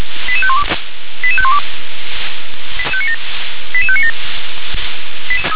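Five-tone selective call (ZVEI), heard as quick runs of five tones, tones of about 70 ms. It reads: 88512 88512 88597
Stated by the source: Five-tone ZVEI selective call received by SDR.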